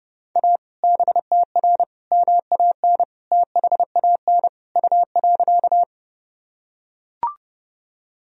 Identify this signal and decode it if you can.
Morse code, heard as a keyed tone, about 32 wpm, 700 Hz. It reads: A BTR MAN THAN U.
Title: Sentences from Top 400 Words and Abbreviations 30wpm